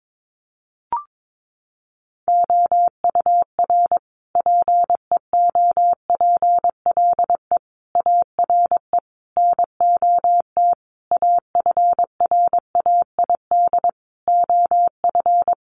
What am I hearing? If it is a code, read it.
Morse code, heard as a keyed tone, about 22 wpm, 700 Hz. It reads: OUR PEOPLE ARE NOT AFRAID OF
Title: Sentences from Top 1500 Words 22wpm